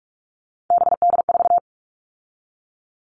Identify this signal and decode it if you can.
Morse code, heard as a keyed tone, about 46 wpm, 700 Hz. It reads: BD4
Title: Callsign Prefixes for Top 100 DXCC - 45wpm